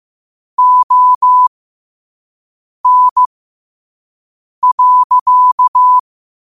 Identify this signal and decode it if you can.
Morse code, heard as a keyed tone, about 15 wpm, spaced as for 5 wpm, 1000 Hz. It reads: ON.
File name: qotdfiles / 5wpm